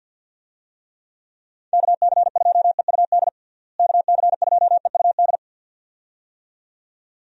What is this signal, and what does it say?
Morse code, heard as a keyed tone, about 52 wpm, 700 Hz. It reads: KC2EUD KC2EUD
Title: USA Call Signs - 6 Characters - Repeated 2x - 50wpm